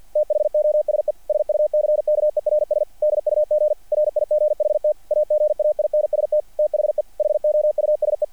Morse code, heard as a keyed tone, about 48 wpm, 660 Hz. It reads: THOSE SUCKERS DUG RIGHT AGAINST THE HOUSE
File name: archived / CRSnet